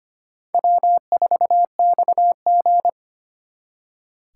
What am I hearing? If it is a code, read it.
Morse code, heard as a keyed tone, about 32 wpm, 700 Hz. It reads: W4XG